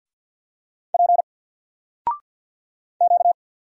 Morse code, heard as a keyed tone, about 48 wpm, 700 Hz. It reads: P Q
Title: Single Character (Letter:Number:Punctuation:Prosign) 50wpm